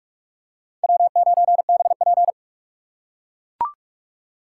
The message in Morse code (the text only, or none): W9BP